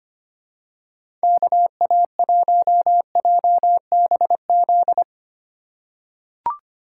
Morse code, heard as a keyed tone, about 25 wpm, 700 Hz. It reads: KA1JBZ
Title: USA Call Signs - 4x Character Spacing - 25wpm